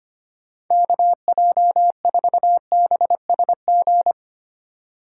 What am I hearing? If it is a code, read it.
Morse code, heard as a keyed tone, about 25 wpm, 700 Hz. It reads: KJ4BSG
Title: USA Call Signs - 6 Characters - 2x Character Spacing - 25wpm